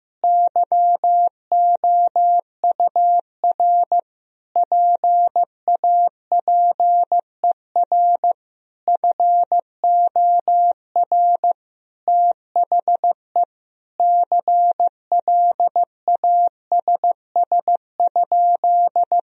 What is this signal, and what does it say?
Morse code, heard as a keyed tone, about 15 wpm, 700 Hz. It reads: YOUR PAPER FOR THE CLASS?
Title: Sentences from Top 1000 Words - Review Part 2 - 15wpm